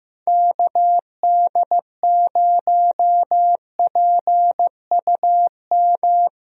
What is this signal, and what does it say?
Morse code, heard as a keyed tone, about 15 wpm, 700 Hz. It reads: KD0PUM
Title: USA Call Signs - 6 Characters - Repeated 3x - 15wpm